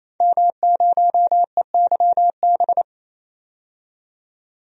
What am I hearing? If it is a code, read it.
Morse code, heard as a keyed tone, about 28 wpm, 700 Hz. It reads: M0EYB
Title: DXCC-100 Call Signs - 5 Characters - 28wpm